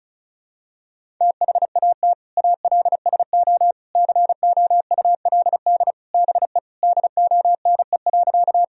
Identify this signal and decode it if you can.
Morse code, heard as a keyed tone, about 34 wpm, 700 Hz. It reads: THAT ALSO COULD BE DONE.
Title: Sentences from Top 1000 Words - Review Part 1 - 35wpm